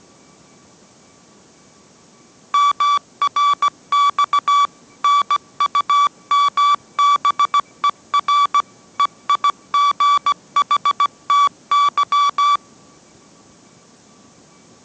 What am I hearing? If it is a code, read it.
Morse code, heard as a keyed tone, about 17 wpm, 1150 Hz. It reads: MRX NUMBER EIGHTY
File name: mrx / mrx080 18wpm